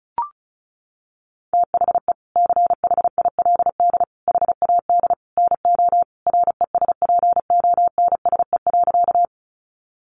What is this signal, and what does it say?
Morse code, heard as a keyed tone, about 34 wpm, 700 Hz. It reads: THE CHILD HAD NO RESPONSE.
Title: Sentences from Top 1400 Words 35wpm